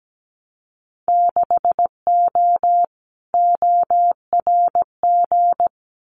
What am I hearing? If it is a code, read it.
Morse code, heard as a keyed tone, about 17 wpm, 700 Hz. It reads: 6O ORG
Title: Winter Field Day Exchanges Encore 17wpm